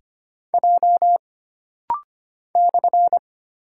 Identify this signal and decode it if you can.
Morse code, heard as a keyed tone, about 25 wpm, 700 Hz. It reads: J /
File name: Single Character (Letter:Number:Punctuation:Prosign) - Rapid-Fire - 25wpm